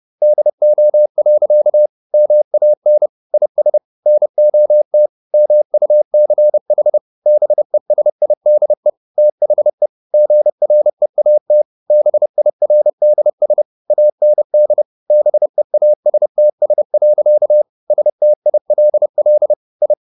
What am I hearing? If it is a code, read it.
Morse code, heard as a keyed tone, about 30 wpm, 600 Hz. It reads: DO. MAN IS NOT MUCH BESIDE THE GREAT BIRDS AND BEASTS. STILL I
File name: the old man and the sea / the old man and the sea 1 30wpm